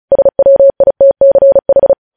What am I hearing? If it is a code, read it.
Morse code, heard as a keyed tone, about 36 wpm, 550 Hz. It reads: SWITCH
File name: podcast / News headlines-1x-35x35